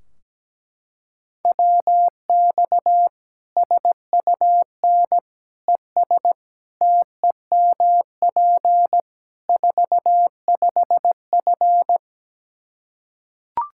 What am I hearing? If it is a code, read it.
Morse code, heard as a keyed tone, about 17 wpm, 700 Hz. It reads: WX SUN ES TEMP 45F
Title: Practice QSO - Exchange 2 - Info - 17wpm